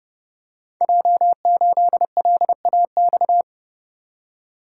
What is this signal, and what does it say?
Morse code, heard as a keyed tone, about 30 wpm, 700 Hz. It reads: J8LAX
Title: Code Groups - 5 Letter:Numbers - 30wpm